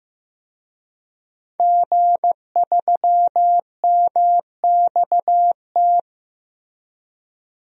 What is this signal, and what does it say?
Morse code, heard as a keyed tone, about 15 wpm, 700 Hz. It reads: G3MXT
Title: DXCC-100 Call Signs - 5 Characters - 15wpm